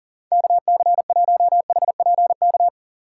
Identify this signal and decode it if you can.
Morse code, heard as a keyed tone, about 40 wpm, 700 Hz. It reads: KC1HPK
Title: USA Call Signs - 6 Characters - 4x Character Spacing - 40wpm